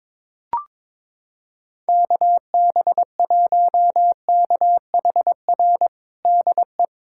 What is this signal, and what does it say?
Morse code, heard as a keyed tone, about 22 wpm, 700 Hz. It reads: KB1KHR DE
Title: USA Callsign DE Callsign - 6 Characters - 22wpm